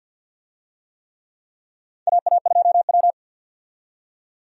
Morse code, heard as a keyed tone, about 46 wpm, 700 Hz. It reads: AA2W
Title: USA Call Signs - 4 Characters - 50wpm